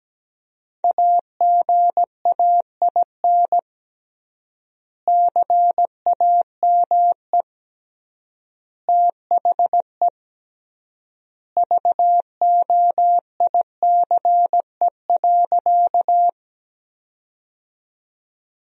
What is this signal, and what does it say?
Morse code, heard as a keyed tone, about 17 wpm, 700 Hz. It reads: AGAIN CAME THE VOICE.
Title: 3x Word-Spacing - Sentences from Top 500 Words - 17wpm